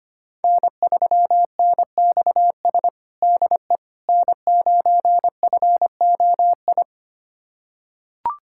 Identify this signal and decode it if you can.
Morse code, heard as a keyed tone, about 25 wpm, 700 Hz. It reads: N3NXS DE N9FOI
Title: USA Callsign DE Callsign - 5 Characters - 25wpm